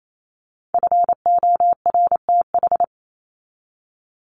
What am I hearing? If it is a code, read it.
Morse code, heard as a keyed tone, about 28 wpm, 700 Hz. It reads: FORTH
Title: Top 1751-2000 Words 28wpm